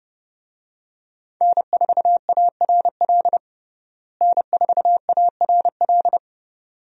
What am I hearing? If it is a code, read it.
Morse code, heard as a keyed tone, about 30 wpm, 700 Hz. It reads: N4ARL N4ARL
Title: USA Call Signs - 5 Characters - Repeated 2x - 30wpm